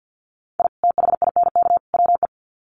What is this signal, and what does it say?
Morse code, heard as a keyed tone, about 50 wpm, 700 Hz. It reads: I THINK WE
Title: Sets of 3 Words 50wpm